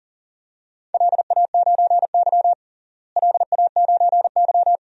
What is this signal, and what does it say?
Morse code, heard as a keyed tone, about 37 wpm, 700 Hz. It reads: LA9Y LA9Y
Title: DXCC-100 Call Signs - 4 Characters - Repeated 2x - 40wpm